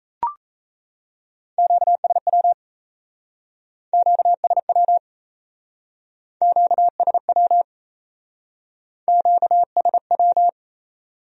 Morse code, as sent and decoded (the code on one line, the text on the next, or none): --.- ... .-- / --.- ... .-- / --.- ... .-- / --.- ... .--
QSW QSW QSW QSW